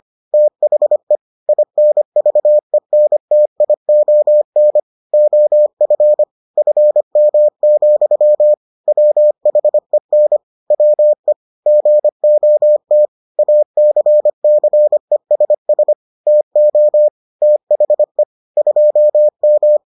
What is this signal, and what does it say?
Morse code, heard as a keyed tone, about 25 wpm, 600 Hz. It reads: THE INVENTION OF FM, WHEN WE GOT ACCESS TO THE 2M